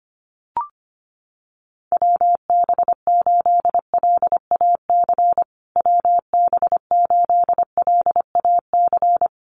Morse code, heard as a keyed tone, about 25 wpm, 700 Hz. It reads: WB8LAC WB8LAC
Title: USA Call Signs Repeated 2x 25wpm